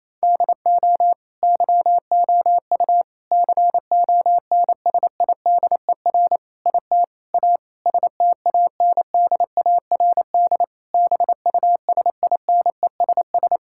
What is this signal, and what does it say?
Morse code, heard as a keyed tone, about 28 wpm, 700 Hz. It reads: DO YOU CONSIDER IT A STANDARD BUSINESS
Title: Sentences from Top 1000 Words - Review Part 6 - 28wpm